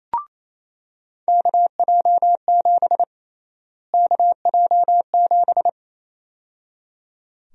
Morse code, heard as a keyed tone, about 27 wpm, 700 Hz. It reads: KJ7 KJ7
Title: USA Call Sign Prefixes - Repeated 2x - 28wpm